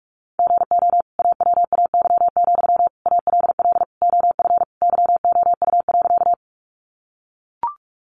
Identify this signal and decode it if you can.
Morse code, heard as a keyed tone, about 44 wpm, 700 Hz. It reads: GO AWAY, ALL OF YOU.
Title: Sentences from Top 600 Words 45wpm